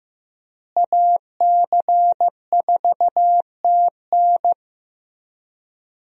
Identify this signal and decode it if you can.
Morse code, heard as a keyed tone, about 15 wpm, 700 Hz. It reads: AC4TN
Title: USA Call Signs - 5 Characters - Repeated 3x - 15wpm